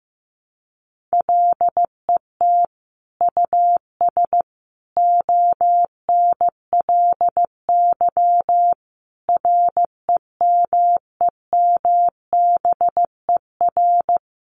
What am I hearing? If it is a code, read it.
Morse code, heard as a keyed tone, about 15 wpm, 700 Hz. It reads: LET US ONLY REMEMBER
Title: Sentences from Top 800 Words 15wpm